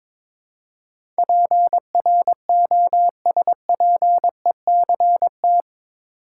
Morse code, heard as a keyed tone, about 22 wpm, 700 Hz. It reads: PROSPECT
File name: practice / 22wpm / Eight Letter Words - 22wpm